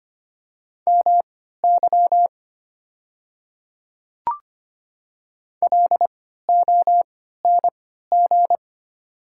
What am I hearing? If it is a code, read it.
Morse code, heard as a keyed tone, about 25 wpm, spaced as for 14 wpm, 700 Hz. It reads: MY LONG
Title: 3x Character Spacing - Top 200 Words - 25wpm